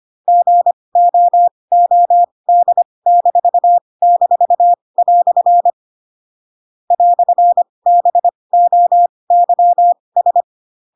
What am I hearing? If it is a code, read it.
Morse code, heard as a keyed tone, about 25 wpm, spaced as for 20 wpm, 700 Hz. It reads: GOOD--" "BOYS
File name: Pinocchio / Pinocchio ch8 15 25 ik1hge